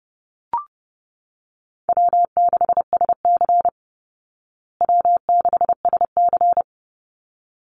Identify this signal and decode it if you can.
Morse code, heard as a keyed tone, about 30 wpm, 700 Hz. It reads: W6SC W6SC
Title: Call Signs Repeated 3x - 30wpm